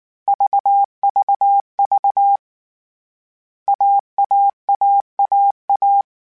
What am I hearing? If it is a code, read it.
Morse code, heard as a keyed tone, about 19 wpm, 800 Hz. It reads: VVV AAAAA